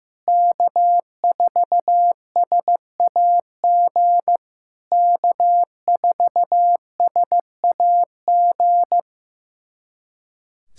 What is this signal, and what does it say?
Morse code, heard as a keyed tone, about 16 wpm, 700 Hz. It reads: K4SAG K4SAG
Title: SOTA QSO Practice 15wpm